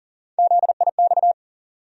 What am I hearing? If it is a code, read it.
Morse code, heard as a keyed tone, about 40 wpm, 700 Hz. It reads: ZIX
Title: USA Call Sign Postfixes - 3 Letters 40wpm